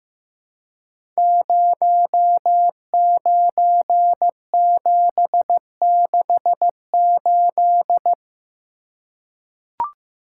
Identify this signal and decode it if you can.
Morse code, heard as a keyed tone, about 15 wpm, 700 Hz. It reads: 09768
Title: Code Groups - 5 Letter:Numbers - 15wpm